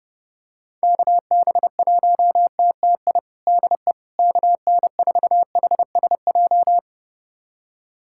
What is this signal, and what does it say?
Morse code, heard as a keyed tone, about 30 wpm, 700 Hz. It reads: KB1TTI DE KN4HSJ
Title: USA Callsign DE Callsign 30wpm